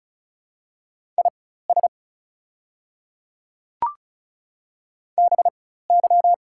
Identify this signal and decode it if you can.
Morse code, heard as a keyed tone, about 34 wpm, spaced as for 16 wpm, 700 Hz. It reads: IS BY
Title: 4x Character Spacing - Top 100 Words - 35wpm